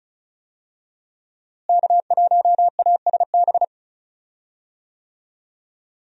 Morse code, heard as a keyed tone, about 36 wpm, 700 Hz. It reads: K1ASB
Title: USA Call Signs - 5 Characters - Repeated 3x - 35wpm